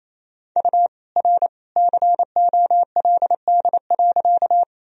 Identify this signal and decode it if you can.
Morse code, heard as a keyed tone, about 28 wpm, 700 Hz. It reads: U R COLD.
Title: Sentences from Top 400 Words and Abbreviations 28wpm